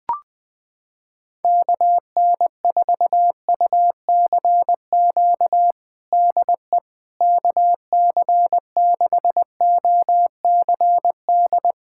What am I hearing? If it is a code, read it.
Morse code, heard as a keyed tone, about 20 wpm, 700 Hz. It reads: KN4UCQ DE KC6OCD
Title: USA Callsign DE Callsign - 6 Characters - 20wpm